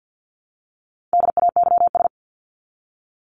Morse code, heard as a keyed tone, about 50 wpm, 700 Hz. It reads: DAYS
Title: ES:S Suffix - Top 100 Words with Base Words 50wpm